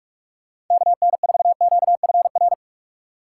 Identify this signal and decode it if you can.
Morse code, heard as a keyed tone, about 44 wpm, 700 Hz. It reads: KN4QFR